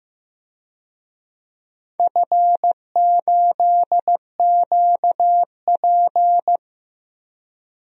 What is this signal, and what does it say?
Morse code, heard as a keyed tone, about 15 wpm, 700 Hz. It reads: F8QP
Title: DXCC-100 Call Signs - 4 Characters - Repeated 3x - 15wpm